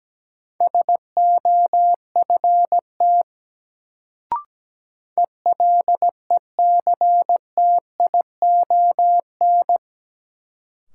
Morse code, heard as a keyed tone, about 17 wpm, 700 Hz. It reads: SOFT ELECTION
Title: Sentences from Top 1500 Words - New Word Drill - 17wpm